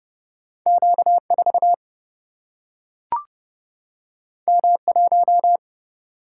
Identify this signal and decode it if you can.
Morse code, heard as a keyed tone, about 30 wpm, 700 Hz. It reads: Q4 M1
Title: Single-Letter Single-Number 30wpm